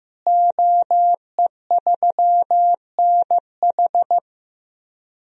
Morse code, heard as a keyed tone, about 15 wpm, 700 Hz. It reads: OE3NH
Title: DXCC-100 Call Signs - 5 Characters - Repeated 2x - 15wpm